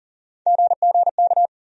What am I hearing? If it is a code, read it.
Morse code, heard as a keyed tone, about 39 wpm, 700 Hz. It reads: GGK
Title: USA Call Sign Postfixes - 3 Letters 40wpm